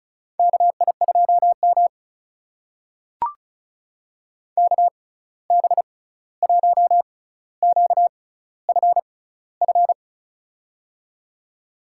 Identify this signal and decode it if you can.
Morse code, heard as a keyed tone, about 34 wpm, spaced as for 16 wpm, 700 Hz. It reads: KI2M KB1QFF
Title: USA Call Signs - 6x Character Spacing - 35wpm